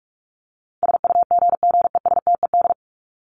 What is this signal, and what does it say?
Morse code, heard as a keyed tone, about 44 wpm, 700 Hz. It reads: SUGGESTED
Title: ED Suffix - Top 100 Words with Base Words 45wpm